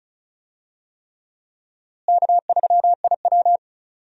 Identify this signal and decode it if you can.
Morse code, heard as a keyed tone, about 36 wpm, 700 Hz. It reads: K3IW
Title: USA Call Signs - 4 Characters - Repeated 3x - 35wpm